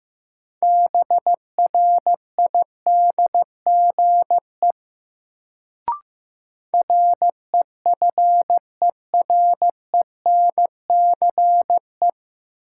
Morse code, heard as a keyed tone, about 16 wpm, 700 Hz. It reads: BRIDGE REFERENCE
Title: Sentences from Top 1200 Words - New Word Drill - 15wpm